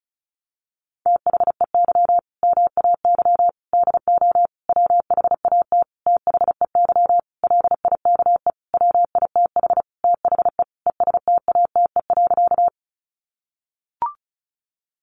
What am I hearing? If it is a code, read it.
Morse code, heard as a keyed tone, about 34 wpm, 700 Hz. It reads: THEY MAY DO WHAT THEY LIKE WITH THE ESTATE.
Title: Sentences from Top 1200 Words 35wpm